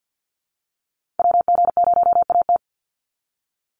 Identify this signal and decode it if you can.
Morse code, heard as a keyed tone, about 50 wpm, 700 Hz. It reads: WG0AT
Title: NA SOTA Activator Callsigns 50wpm